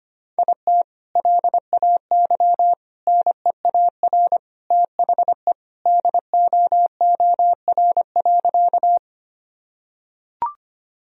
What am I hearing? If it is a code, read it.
Morse code, heard as a keyed tone, about 25 wpm, 700 Hz. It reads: IT LAY NEAR THE DOOR.
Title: Sentences from Top 1000 Words - Review Part 2 - 25wpm